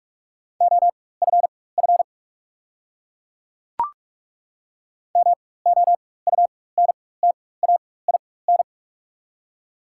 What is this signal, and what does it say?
Morse code, heard as a keyed tone, about 43 wpm, spaced as for 21 wpm, 700 Hz. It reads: OFF MOUNTAIN
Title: 4x Character Spacing - Top 300 Words - 45wpm